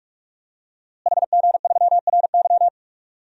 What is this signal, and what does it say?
Morse code, heard as a keyed tone, about 41 wpm, 700 Hz. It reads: HG3RY
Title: DXCC-100 Call Signs - 5 Characters - Repeated 3x - 45wpm